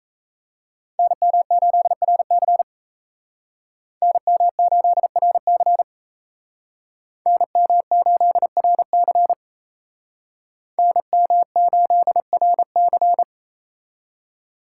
Morse code, heard as a keyed tone, about 34 wpm, 700 Hz. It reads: NM8RC NM8RC NM8RC NM8RC
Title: Speed Racing - USA 2x2 Call Signs 28wpm